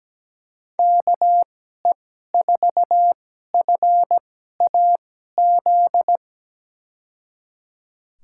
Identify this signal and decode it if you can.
Morse code, heard as a keyed tone, about 17 wpm, spaced as for 12 wpm, 700 Hz. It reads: KE4FAZ
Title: USA Call Signs - 6 Characters - 2x Character Spacing - 17wpm